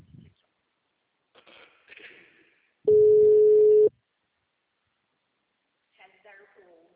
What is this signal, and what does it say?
Ringback tone (Europe, 425 Hz), with speech.